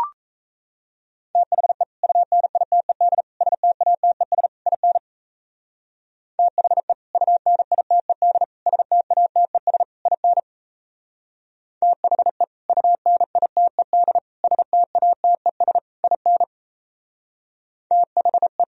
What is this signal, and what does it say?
Morse code, heard as a keyed tone, about 37 wpm, 700 Hz. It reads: THE UNITED STATES IN THE UNITED STATES IN THE UNITED STATES IN THE